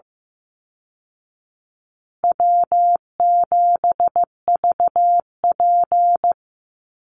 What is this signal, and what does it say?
Morse code, heard as a keyed tone, about 15 wpm, 700 Hz. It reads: W7VP
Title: Call Signs Repeated 3x - 15wpm